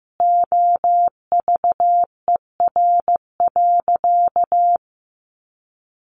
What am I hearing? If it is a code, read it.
Morse code, heard as a keyed tone, about 15 wpm, 700 Hz. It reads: OVER.
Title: Sentences from Top 1100 Words 15wpm